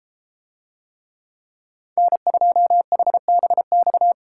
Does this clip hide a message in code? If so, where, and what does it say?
Yes: Morse code from 2.0 to 4.1 s, heard as a keyed tone, about 30 wpm, 700 Hz. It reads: N2HBX